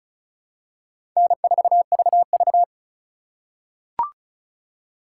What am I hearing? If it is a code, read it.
Morse code, heard as a keyed tone, about 36 wpm, 700 Hz. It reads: N4VV
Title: USA Call Signs - 4 Characters - 6x Character Spacing - 35wpm